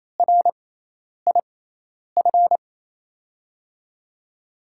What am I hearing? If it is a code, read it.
Morse code, heard as a keyed tone, about 27 wpm, spaced as for 10 wpm, 700 Hz. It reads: RIF
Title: USA Call Signs - 6 Characters - 6x Character Spacing - 28wpm